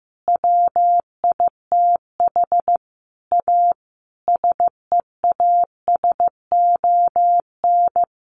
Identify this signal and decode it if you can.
Morse code, heard as a keyed tone, about 15 wpm, 700 Hz. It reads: WITH A SEASON
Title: Sentences from Top 1400 Words 15wpm